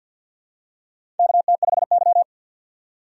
Morse code, heard as a keyed tone, about 47 wpm, 700 Hz. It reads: KT5Y
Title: USA Call Signs - 4 Characters - 50wpm